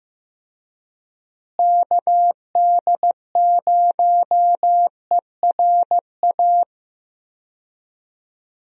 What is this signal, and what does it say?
Morse code, heard as a keyed tone, about 15 wpm, 700 Hz. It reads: KD0ERA
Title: USA Call Signs - 6 Characters - Repeated 2x - 15wpm